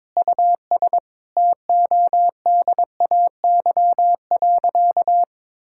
Morse code, heard as a keyed tone, about 22 wpm, 700 Hz. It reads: US TODAY.
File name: Sentences with 7-Word Phrases - 22wpm